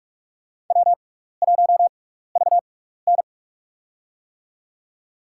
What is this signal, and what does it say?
Morse code, heard as a keyed tone, about 43 wpm, spaced as for 15 wpm, 700 Hz. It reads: W1VN
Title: USA Call Signs - 4 Characters - 6x Character Spacing - 45wpm